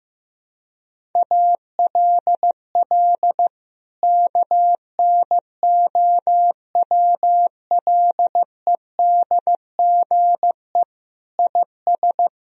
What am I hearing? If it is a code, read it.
Morse code, heard as a keyed tone, about 15 wpm, 700 Hz. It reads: ALL KNOWLEDGE IS
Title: Sentences from Top 1000 Words - Review Part 3 - 15wpm